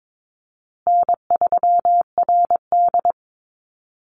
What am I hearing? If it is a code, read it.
Morse code, heard as a keyed tone, about 22 wpm, 700 Hz. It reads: N3RD
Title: Call Signs Repeated 3x - 22wpm